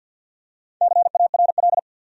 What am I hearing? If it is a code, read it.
Morse code, heard as a keyed tone, about 51 wpm, 700 Hz. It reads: CARL
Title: USA Common Names - Max 4 Letters - 50wpm